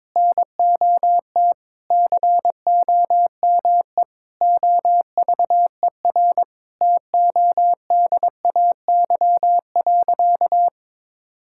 Morse code, heard as a keyed tone, about 21 wpm, 700 Hz. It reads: NOT COME OVER TODAY.
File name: practice / 22wpm / Sentences with 4-Word Phrases - 22wpm